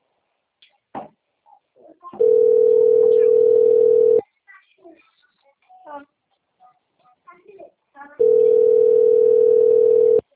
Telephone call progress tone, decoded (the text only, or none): ringback tone